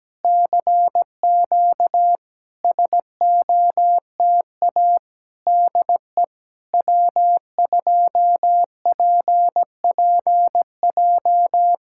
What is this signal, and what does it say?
Morse code, heard as a keyed tone, about 17 wpm, 700 Hz. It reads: CQ SOTA DE W2PPJ
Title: SOTA QSO Practice 17wpm